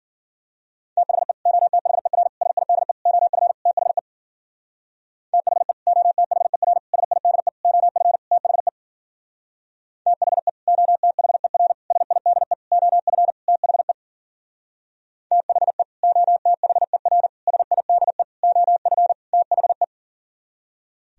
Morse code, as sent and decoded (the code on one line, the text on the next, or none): - .... / --- - .... .-. / ... .. -.. / --- ..-. / - .... / - .... / --- - .... .-. / ... .. -.. / --- ..-. / - .... / - .... / --- - .... .-. / ... .. -.. / --- ..-. / - .... / - .... / --- - .... .-. / ... .. -.. / --- ..-. / - .... .
THE OTHER SIDE OF THE THE OTHER SIDE OF THE THE OTHER SIDE OF THE THE OTHER SIDE OF THE